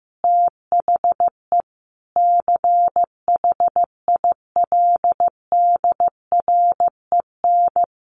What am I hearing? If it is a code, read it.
Morse code, heard as a keyed tone, about 15 wpm, 700 Hz. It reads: THE CHILDREN
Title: Sentences from Top 1300 Words 15wpm